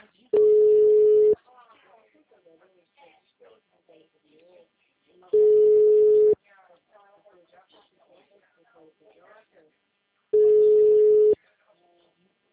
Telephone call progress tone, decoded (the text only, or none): ringback tone